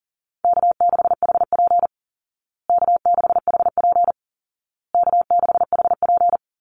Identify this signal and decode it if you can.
Morse code, heard as a keyed tone, about 40 wpm, 700 Hz. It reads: K6HP K6HP K6HP
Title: Call Signs Repeated 2x - 40wpm